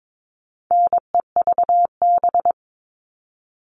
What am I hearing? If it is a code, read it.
Morse code, heard as a keyed tone, about 22 wpm, 700 Hz. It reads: NEVB
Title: All Cut Numbers - 4-Digit 22wpm